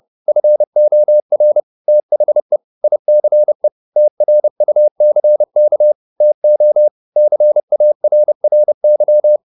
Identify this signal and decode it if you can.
Morse code, heard as a keyed tone, about 30 wpm, 600 Hz. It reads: FOR THE ICE TRUCK TO CARRY
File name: the old man and the sea / the old man and the sea 1 30wpm